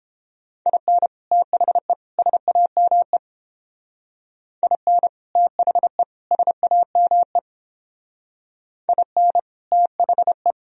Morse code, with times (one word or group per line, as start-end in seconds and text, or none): 0.7-1.1 IN
1.3-1.9 THE
2.2-3.2 SAME
4.6-5.1 IN
5.4-6.0 THE
6.3-7.4 SAME
8.9-9.4 IN
9.7-10.5 THE